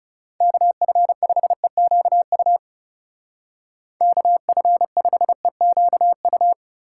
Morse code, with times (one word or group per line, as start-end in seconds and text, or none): 0.4-2.6 KF5EQU
4.0-6.5 KF5EQU